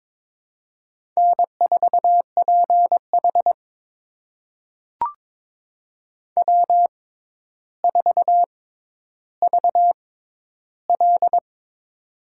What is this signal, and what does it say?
Morse code, heard as a keyed tone, about 22 wpm, spaced as for 11 wpm, 700 Hz. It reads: N4PH W4VL
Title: USA Call Signs - 4 Characters - 6x Character Spacing - 22wpm